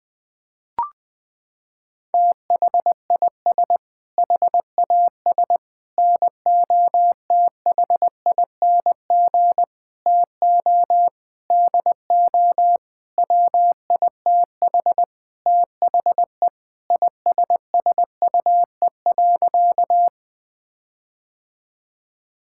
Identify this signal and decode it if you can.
Morse code, heard as a keyed tone, about 19 wpm, 700 Hz. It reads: THIS HAS NOTHING TO DO WITH THE ISSUE.
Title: Sentences with 6-Word Phrases - 20wpm